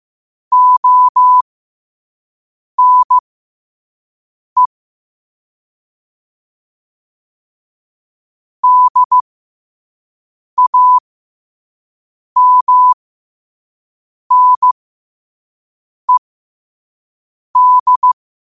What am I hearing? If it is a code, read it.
Morse code, heard as a keyed tone, about 15 wpm, spaced as for 5 wpm, 1000 Hz. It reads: ONE DAMNED
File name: 5wpm